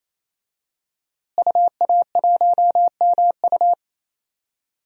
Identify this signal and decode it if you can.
Morse code, heard as a keyed tone, about 28 wpm, 700 Hz. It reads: UA1MU
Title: DXCC-100 Call Signs - 5 Characters - Repeated 3x - 28wpm